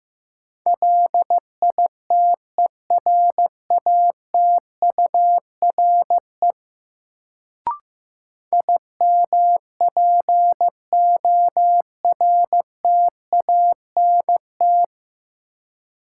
Morse code, heard as a keyed tone, about 15 wpm, 700 Hz. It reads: LITERATURE IMPORTANT
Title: Sentences from Top 600 Words - New Word Drill - 15wpm